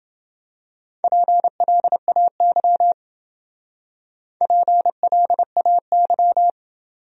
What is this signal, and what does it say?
Morse code, heard as a keyed tone, about 28 wpm, 700 Hz. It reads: PLAY PLAY